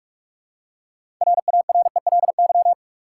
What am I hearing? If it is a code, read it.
Morse code, heard as a keyed tone, about 46 wpm, 700 Hz. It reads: RARELY
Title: Six Letter Words - 45wpm